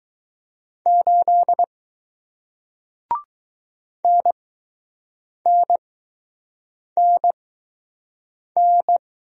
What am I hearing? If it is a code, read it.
Morse code, heard as a keyed tone, about 20 wpm, spaced as for 6 wpm, 700 Hz. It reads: 8 NNNN